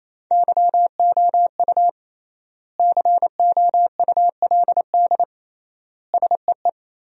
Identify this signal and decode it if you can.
Morse code, heard as a keyed tone, about 28 wpm, 700 Hz. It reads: YOU COULD SEE
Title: 3x Word-Spacing - Sentences from Top 500 Words - 28wpm